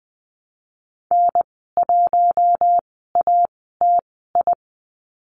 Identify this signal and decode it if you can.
Morse code, heard as a keyed tone, about 20 wpm, spaced as for 14 wpm, 700 Hz. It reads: N1ATI
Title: 042 Lesson 8g - Number 1 - Call Signs 20wpm